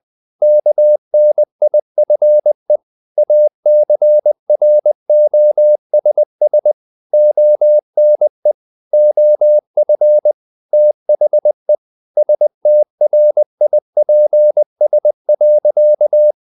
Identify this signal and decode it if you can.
Morse code, heard as a keyed tone, about 20 wpm, 600 Hz. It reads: KNIFE ACROSS ONE OF THE STRIPS.
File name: the old man and the sea / the old man and the sea part 1 20wpm